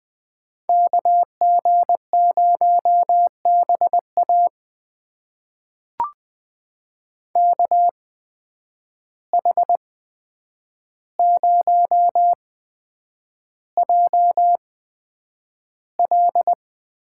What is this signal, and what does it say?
Morse code, heard as a keyed tone, about 20 wpm, spaced as for 9 wpm, 700 Hz. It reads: KG0BA KH0JL
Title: USA Call Signs - 5 Characters - 8x Character Spacing - 20wpm